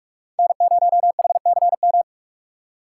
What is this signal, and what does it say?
Morse code, heard as a keyed tone, about 46 wpm, 700 Hz. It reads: N0HCM
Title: USA Call Signs - 5 Characters - 3x Character Spacing - 45wpm